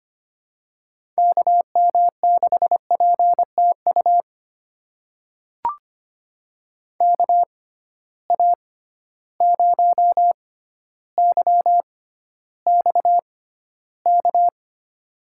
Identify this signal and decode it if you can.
Morse code, heard as a keyed tone, about 25 wpm, spaced as for 13 wpm, 700 Hz. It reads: KM6PTU KA0YXK
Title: USA Call Signs - 6 Characters - 6x Character Spacing - 25wpm